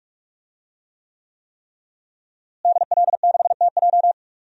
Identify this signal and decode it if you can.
Morse code, heard as a keyed tone, about 41 wpm, 700 Hz. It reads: DL6TJ
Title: DXCC-100 Call Signs - 5 Characters - Repeated 3x - 45wpm